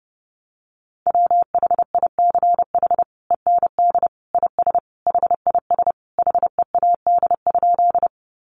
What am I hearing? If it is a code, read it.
Morse code, heard as a keyed tone, about 30 wpm, 700 Hz. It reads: WHICH END IS HIS HEAD?
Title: Sentences from Top 300 Words 30wpm